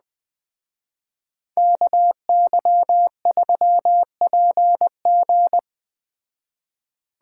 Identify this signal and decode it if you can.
Morse code, heard as a keyed tone, about 20 wpm, 700 Hz. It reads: KY3PG
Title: Speed Racing - USA 2x2 Call Signs 17wpm